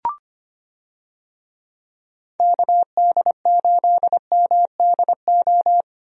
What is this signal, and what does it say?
Morse code, heard as a keyed tone, about 25 wpm, 700 Hz. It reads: KD8MDO
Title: USA Call Signs - 6 Characters - 25wpm